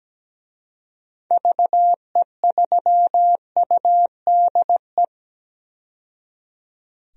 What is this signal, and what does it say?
Morse code, heard as a keyed tone, about 17 wpm, 700 Hz. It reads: VE3UDE
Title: DXCC-100 Call Signs - 6 Characters - 17wpm